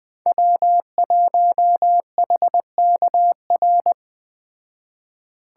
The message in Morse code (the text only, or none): W1HKR